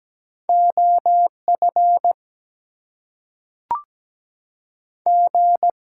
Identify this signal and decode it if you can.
Morse code, heard as a keyed tone, about 17 wpm, 700 Hz. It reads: OF G
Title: Two Letter Words - 17wpm